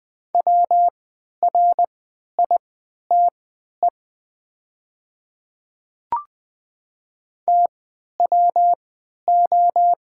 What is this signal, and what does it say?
Morse code, heard as a keyed tone, about 20 wpm, spaced as for 11 wpm, 700 Hz. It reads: WRITE TWO
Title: 3x Character Spacing - Top 100 Words - 20wpm